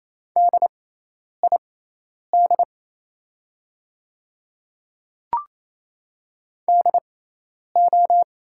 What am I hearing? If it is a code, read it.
Morse code, heard as a keyed tone, about 28 wpm, spaced as for 10 wpm, 700 Hz. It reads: DID DO
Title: 6x Character Spacing - Top 100 Words - 28wpm